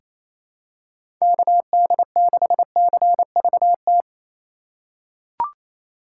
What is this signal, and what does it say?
Morse code, heard as a keyed tone, about 28 wpm, 700 Hz. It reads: KD6CVT
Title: USA Call Signs - 6 Characters - 4x Character Spacing - 28wpm